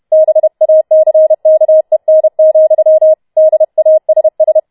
Morse code, heard as a keyed tone, about 31 wpm, 630 Hz. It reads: BACKEN, DASS